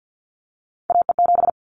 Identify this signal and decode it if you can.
Morse code, heard as a keyed tone, about 50 wpm, 700 Hz. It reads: AE7
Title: Callsign Prefixes for Top 25 DXCC - 50wpm